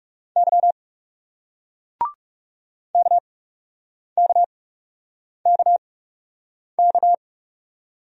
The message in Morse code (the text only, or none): Y KKKK